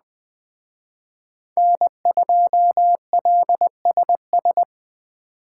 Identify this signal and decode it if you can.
Morse code, heard as a keyed tone, about 21 wpm, 700 Hz. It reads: N2LSS
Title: Speed Racing - USA 1x3 Call Signs 17wpm